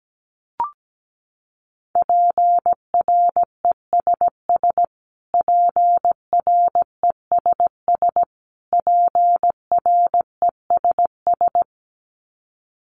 Morse code, heard as a keyed tone, about 17 wpm, 700 Hz. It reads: PRESS PRESS PRESS
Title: Top 300 Words Repeated 17wpm